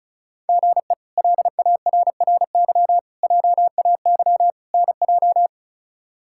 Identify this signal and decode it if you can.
Morse code, heard as a keyed tone, about 36 wpm, 700 Hz. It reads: GE LARRY JAY NJ
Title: K1USN SST QSO Practice - 35wpm